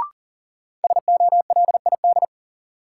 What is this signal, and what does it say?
Morse code, heard as a keyed tone, about 40 wpm, 700 Hz. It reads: SOLID
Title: Sentences from Top 1500 Words - New Word Drill Alt - 40wpm